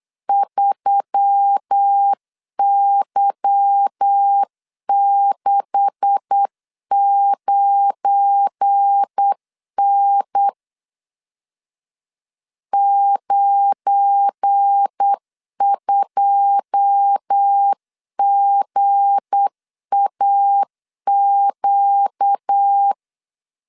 Morse code, heard as a keyed tone, about 8 wpm, 800 Hz. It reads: 3Y69N 92GAQ
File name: LEZIONI / 9 286 8wpm 3425